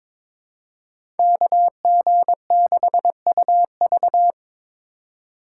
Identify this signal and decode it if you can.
Morse code, heard as a keyed tone, about 21 wpm, 700 Hz. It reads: KG6UV
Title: USA Call Signs - 5 Characters - 3x Character Spacing - 22wpm